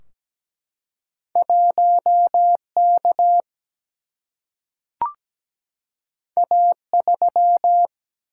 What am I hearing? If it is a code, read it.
Morse code, heard as a keyed tone, about 17 wpm, 700 Hz. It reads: 1 K A 3
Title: Single-Letter Single-Number 17wpm